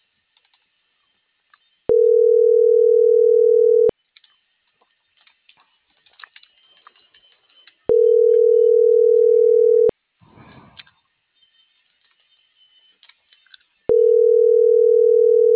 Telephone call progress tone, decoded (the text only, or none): ringback tone